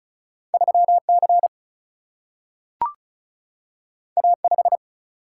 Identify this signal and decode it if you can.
Morse code, heard as a keyed tone, about 36 wpm, 700 Hz. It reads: 3C A5
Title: Single-Letter Single-Number 35wpm